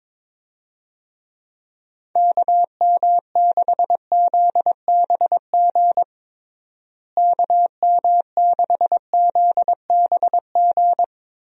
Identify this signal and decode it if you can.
Morse code, heard as a keyed tone, about 22 wpm, 700 Hz. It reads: KM6ZBG KM6ZBG
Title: USA Call Signs - 6 Characters - Repeated 2x - 22wpm